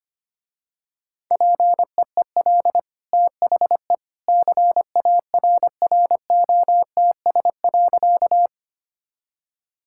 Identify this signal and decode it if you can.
Morse code, heard as a keyed tone, about 25 wpm, 700 Hz. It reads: PEEL THE CARROTS.
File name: Easy Sentences with Hints - 25wpm